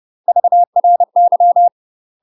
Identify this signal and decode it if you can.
Morse code, heard as a keyed tone, about 30 wpm, 700 Hz. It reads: VRY
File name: UK MASTER QSOs TRAINING / 30WPM / UK MASTER QSOs 0036